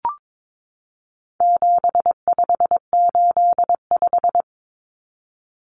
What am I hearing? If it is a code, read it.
Morse code, heard as a keyed tone, about 22 wpm, 700 Hz. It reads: 7585
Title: Common Cut Numbers - 4-Digit 22wpm